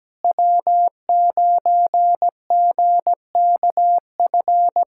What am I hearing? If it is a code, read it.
Morse code, heard as a keyed tone, about 17 wpm, 700 Hz. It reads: W9GKF
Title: USA Callsign DE Callsign - 5 Characters - 17wpm